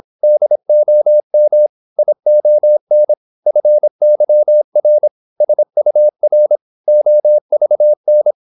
Morse code, heard as a keyed tone, about 26 wpm, 600 Hz. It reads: DOM ION FYR SUR OVN
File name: Korte ord DK 26wpm